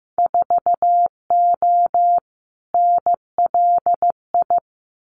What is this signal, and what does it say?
Morse code, heard as a keyed tone, about 15 wpm, 700 Hz. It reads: 4O NLI
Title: Winter Field Day Exchanges Encore 15wpm